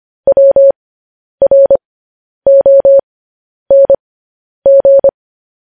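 Morse code, heard as a keyed tone, about 25 wpm, spaced as for 10 wpm, 550 Hz. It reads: WRONG
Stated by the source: Morse code transcription of news headlines sent at 10 wpm.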